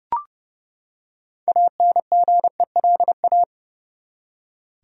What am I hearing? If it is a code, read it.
Morse code, heard as a keyed tone, about 31 wpm, 700 Hz. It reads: ANGELA
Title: USA Common Names - 30wpm